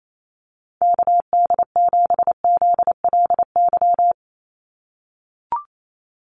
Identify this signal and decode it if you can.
Morse code, heard as a keyed tone, about 28 wpm, 700 Hz. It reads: KD7ZLY
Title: USA Call Signs 28wpm